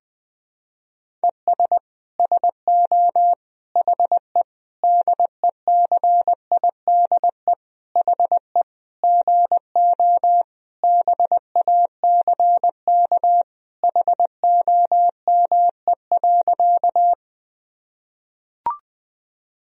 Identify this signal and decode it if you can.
Morse code, heard as a keyed tone, about 20 wpm, 700 Hz. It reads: ES SO HE DECIDE HE GO BACK HOME.
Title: Sentences from Top 500 Words and Abbreviations 20wpm